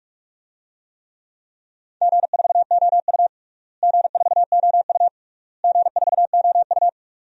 Morse code, heard as a keyed tone, about 46 wpm, 700 Hz. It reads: G4OU G4OU G4OU
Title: DXCC-100 Call Signs - 4 Characters - Repeated 3x - 45wpm